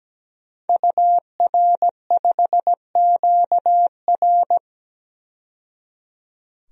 Morse code, heard as a keyed tone, about 17 wpm, 700 Hz. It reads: UR5QR
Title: DXCC-100 Call Signs - 5 Characters - 17wpm